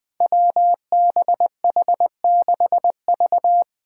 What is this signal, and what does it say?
Morse code, heard as a keyed tone, about 20 wpm, 700 Hz. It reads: WBH6V
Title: Five Challenging Letter-Number Combinations 20wpm